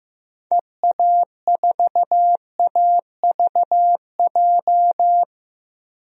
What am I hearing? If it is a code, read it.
Morse code, heard as a keyed tone, about 15 wpm, 700 Hz. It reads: EA4AVJ
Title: DXCC-100 Call Signs - 6 Characters - 15wpm